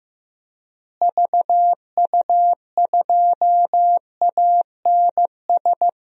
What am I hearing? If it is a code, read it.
Morse code, heard as a keyed tone, about 15 wpm, 700 Hz. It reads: VU2ANS
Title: DXCC-100 Call Signs - 6 Characters - Repeated 3x - 15wpm